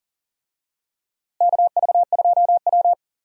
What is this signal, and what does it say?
Morse code, heard as a keyed tone, about 40 wpm, 700 Hz. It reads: KV2W
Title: USA Call Signs - 4 Characters - Repeated 2x - 40wpm